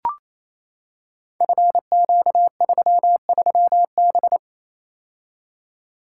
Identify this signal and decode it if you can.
Morse code, heard as a keyed tone, about 28 wpm, 700 Hz. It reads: FQ33B
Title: Code Groups - 5 Letter:Numbers - 28wpm